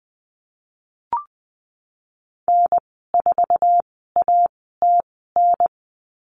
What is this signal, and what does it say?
Morse code, heard as a keyed tone, about 20 wpm, spaced as for 14 wpm, 700 Hz. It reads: N4ATN
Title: 069 Lesson 12g - Letter d - Call Signs 20wpm